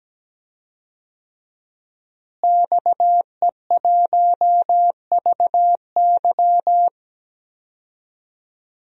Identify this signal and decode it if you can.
Morse code, heard as a keyed tone, about 17 wpm, 700 Hz. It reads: XE1VY
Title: DXCC-100 Call Signs - 5 Characters - Repeated 2x - 17wpm